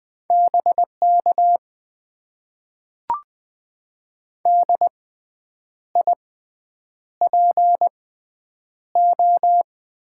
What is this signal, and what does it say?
Morse code, heard as a keyed tone, about 20 wpm, spaced as for 8 wpm, 700 Hz. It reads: BK DIPO
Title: Top 90 QSO Elements - 6x Character Spacing - 20wpm